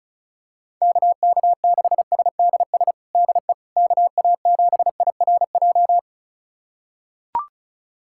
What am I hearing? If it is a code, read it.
Morse code, heard as a keyed tone, about 36 wpm, 700 Hz. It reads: KK6SDS DE KA7IRJ
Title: USA Callsign DE Callsign - 6 Characters - 35wpm